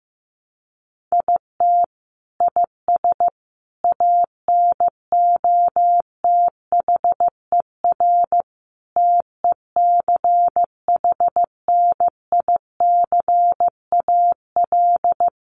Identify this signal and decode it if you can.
Morse code, heard as a keyed tone, about 15 wpm, 700 Hz. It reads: IT IS ANOTHER TECHNICAL
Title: Sentences from Top 1100 Words 15wpm